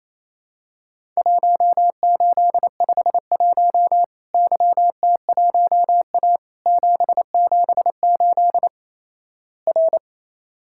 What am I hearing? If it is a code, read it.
Morse code, heard as a keyed tone, about 28 wpm, 700 Hz. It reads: 1851 YT1A 778 R
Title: Worked All Europe DX Contest - QTC Exchange - 28wpm